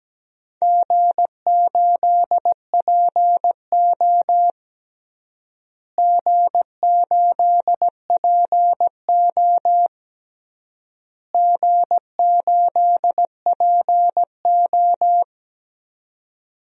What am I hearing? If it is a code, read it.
Morse code, heard as a keyed tone, about 17 wpm, 700 Hz. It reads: G8PO G8PO G8PO
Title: DXCC-100 Call Signs - 4 Characters - Repeated 3x - 17wpm